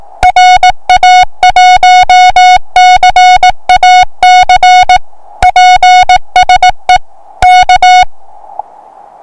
Morse code, heard as a keyed tone, about 18 wpm, 730 Hz. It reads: RA1CAC PSE K
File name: attachments / 1cq noagc 2cq agcin 3cq agcout